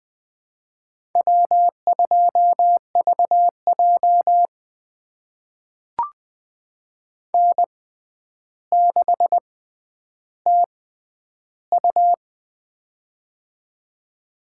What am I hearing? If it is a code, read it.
Morse code, heard as a keyed tone, about 20 wpm, spaced as for 10 wpm, 700 Hz. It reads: W2VJ N6TU
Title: USA Call Signs - 4 Characters - 6x Character Spacing - 20wpm